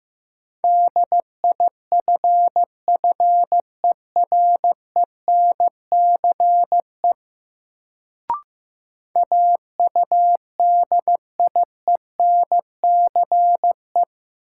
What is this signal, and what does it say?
Morse code, heard as a keyed tone, about 16 wpm, 700 Hz. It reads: DIFFERENCE AUDIENCE
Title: Sentences from Top 800 Words - New Word Drill - 15wpm